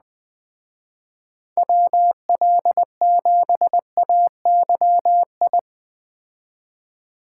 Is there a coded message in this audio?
Yes: Morse code from 1.6 to 5.6 s, heard as a keyed tone, about 20 wpm, 700 Hz. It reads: WL7AYI